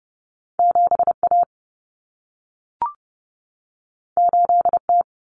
Morse code, heard as a keyed tone, about 30 wpm, 700 Hz. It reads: 7A 8T
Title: Common Cut Numbers - 2-Digit 30wpm